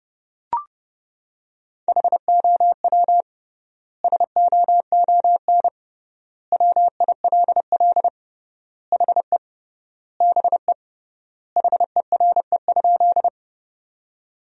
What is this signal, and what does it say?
Morse code, heard as a keyed tone, about 30 wpm, 700 Hz. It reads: HOW SOON WILL HE BE HERE?
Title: 3x Word-Spacing - Sentences from Top 400 Words - 30wpm